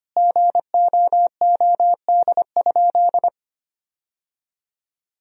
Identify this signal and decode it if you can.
Morse code, heard as a keyed tone, about 25 wpm, 700 Hz. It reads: GOOD?
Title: Sentences from Top 1000 Words - Review Part 3 - 25wpm